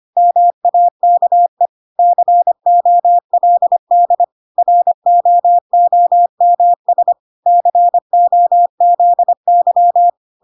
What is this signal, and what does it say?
Morse code, heard as a keyed tone, about 25 wpm, 700 Hz. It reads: MAKE COLD ROOMS COZY
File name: Pinocchio / Pinocchio ch1 25 25 ik1hge